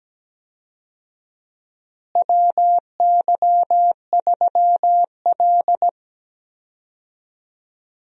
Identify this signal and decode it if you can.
Morse code, heard as a keyed tone, about 17 wpm, 700 Hz. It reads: WY3L
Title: USA Call Signs - 4 Characters - Repeated 3x - 17wpm